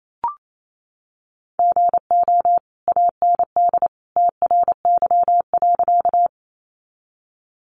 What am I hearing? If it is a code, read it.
Morse code, heard as a keyed tone, about 28 wpm, 700 Hz. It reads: GO AND TRY.
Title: Sentences from Top 200 Words 28wpm